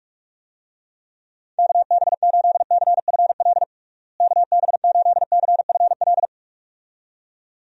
Morse code, heard as a keyed tone, about 46 wpm, 700 Hz. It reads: KB8CFL KB8CFL
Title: USA Call Signs - 6 Characters - Repeated 2x - 45wpm